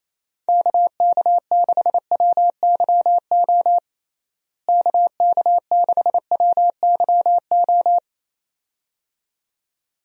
Morse code, heard as a keyed tone, about 28 wpm, 700 Hz. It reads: KK6WYO KK6WYO
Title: USA Call Signs - 6 Characters - Repeated 2x - 28wpm